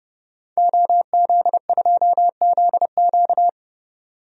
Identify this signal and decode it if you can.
Morse code, heard as a keyed tone, about 30 wpm, 700 Hz. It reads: OZ2ZQ
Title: DXCC-100 Call Signs - 5 Characters - 30wpm